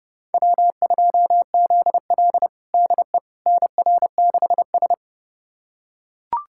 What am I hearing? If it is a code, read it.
Morse code, heard as a keyed tone, about 30 wpm, 700 Hz. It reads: W2ZL DE NR6S
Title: USA Callsign DE Callsign - 4 Characters - 30wpm